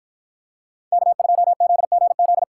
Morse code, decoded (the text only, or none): K2ZGZ